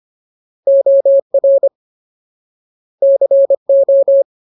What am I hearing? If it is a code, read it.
Morse code, heard as a keyed tone, about 25 wpm, 550 Hz. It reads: OR CO
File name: Ultimate Word List / 25WPM Ultimate